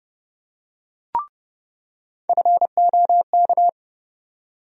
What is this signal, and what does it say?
Morse code, heard as a keyed tone, about 30 wpm, 700 Hz. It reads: FOK
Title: USA Call Sign Postfixes - 3 Letters 30wpm